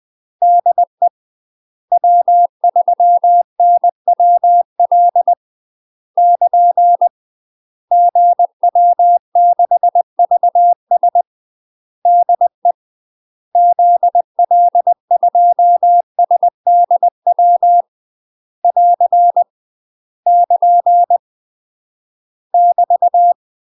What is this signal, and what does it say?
Morse code, heard as a keyed tone, about 20 wpm, 700 Hz. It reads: DE W3NWL <KN> GW6VS DE ZL2SDW <AR> <KN> <BT>
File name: UK MASTER QSOs TRAINING / 20WPM / UK MASTER QSOs 0026